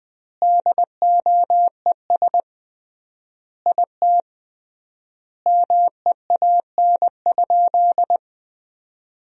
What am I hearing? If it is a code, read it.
Morse code, heard as a keyed tone, about 21 wpm, 700 Hz. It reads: DOES IT MEAN?
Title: 3x Word-Spacing - Sentences from Top 200 Words - 20wpm